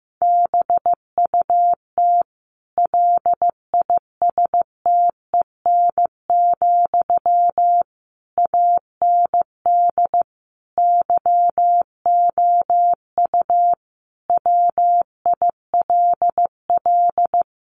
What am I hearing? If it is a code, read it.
Morse code, heard as a keyed tone, about 15 wpm, 700 Hz. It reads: BUT LISTEN, AND YOU WILL
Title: Sentences from Top 600 Words 15wpm